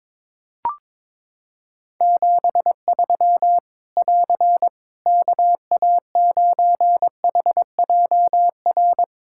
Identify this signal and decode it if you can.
Morse code, heard as a keyed tone, about 22 wpm, 700 Hz. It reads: 73 <AR> KA9HJR
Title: Practice QSO - Exchange 4 - Ending - 22wpm